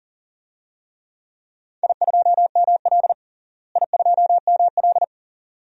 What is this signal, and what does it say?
Morse code, heard as a keyed tone, about 40 wpm, 700 Hz. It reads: I2ML I2ML
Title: DXCC-100 Call Signs - 4 Characters - Repeated 2x - 40wpm